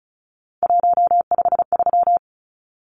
Morse code, heard as a keyed tone, about 34 wpm, 700 Hz. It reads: 153
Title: Numbers - 3-Digit 35wpm